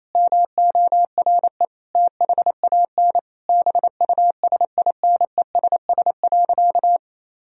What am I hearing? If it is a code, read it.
Morse code, heard as a keyed tone, about 28 wpm, 700 Hz. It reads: MORE THAN BUSINESS.
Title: Sentences from Top 1000 Words - Review Part 2 - 28wpm